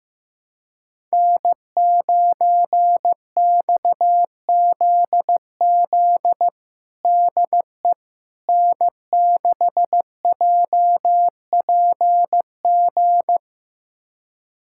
Morse code, heard as a keyed tone, about 15 wpm, 700 Hz. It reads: N9XZZ DE N6JPG
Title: USA Callsign DE Callsign - 5 Characters - 15wpm